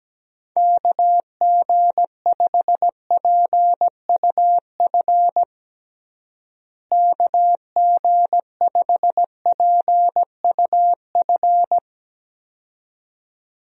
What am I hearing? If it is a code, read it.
Morse code, heard as a keyed tone, about 17 wpm, 700 Hz. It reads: KG5PUF KG5PUF
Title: USA Call Signs - 6 Characters - Repeated 2x - 17wpm